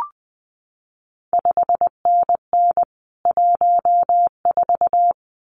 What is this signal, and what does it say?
Morse code, heard as a keyed tone, about 20 wpm, 700 Hz. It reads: 5NN 14
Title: CQ WW DX Contest Exchange - 20wpm